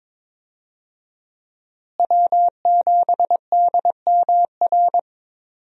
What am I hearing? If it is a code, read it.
Morse code, heard as a keyed tone, about 22 wpm, 700 Hz. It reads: W7DMR
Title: USA Call Signs - 5 Characters - Repeated 3x - 22wpm